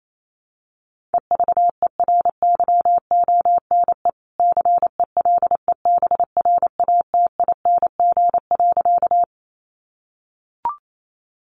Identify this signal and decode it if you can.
Morse code, heard as a keyed tone, about 28 wpm, 700 Hz. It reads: EVERYONE CELEBRATING.
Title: A Sound Of Thunder 28wpm